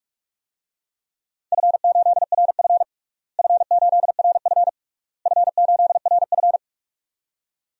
Morse code, heard as a keyed tone, about 47 wpm, 700 Hz. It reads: F8RF F8RF F8RF
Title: DXCC-100 Call Signs - 4 Characters - Repeated 3x - 45wpm